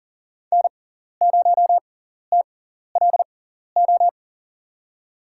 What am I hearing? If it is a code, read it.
Morse code, heard as a keyed tone, about 40 wpm, spaced as for 14 wpm, 700 Hz. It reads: N0TLO
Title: USA Call Signs - 6x Character Spacing - 40wpm